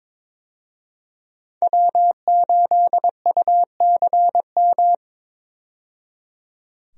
Morse code, heard as a keyed tone, about 23 wpm, 700 Hz. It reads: W8UCM
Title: USA Call Signs - 5 Characters - 22wpm